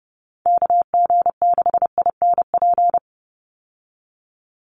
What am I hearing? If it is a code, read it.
Morse code, heard as a keyed tone, about 30 wpm, 700 Hz. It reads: KG6INP
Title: USA Call Signs 30wpm